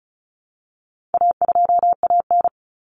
Morse code, heard as a keyed tone, about 34 wpm, 700 Hz. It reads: A2AN
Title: Common Cut Numbers - 4-Digit 35wpm